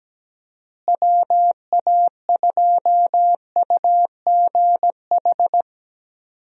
Morse code, heard as a keyed tone, about 17 wpm, 700 Hz. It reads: WA2UGH
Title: USA Call Signs - 6 Characters - Repeated 2x - 17wpm